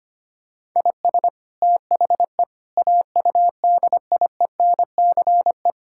Morse code, heard as a keyed tone, about 25 wpm, 700 Hz. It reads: IS THE AUDIENCE
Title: Sentences from Top 1000 Words - Review Part 5 - 25wpm